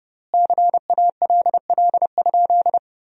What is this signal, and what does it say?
Morse code, heard as a keyed tone, about 30 wpm, 700 Hz. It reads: CALL?
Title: 3x Word-Spacing - Sentences from Top 100 Words - 30wpm